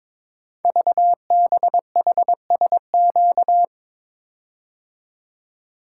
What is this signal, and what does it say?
Morse code, heard as a keyed tone, about 22 wpm, 700 Hz. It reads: VBHSQ
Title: Code Groups - 5 Letter:Numbers - 22wpm